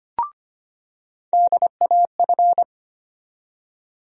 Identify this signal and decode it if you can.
Morse code, heard as a keyed tone, about 25 wpm, 700 Hz. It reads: DAF
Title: USA Call Sign Postfixes - 3 Letters 25wpm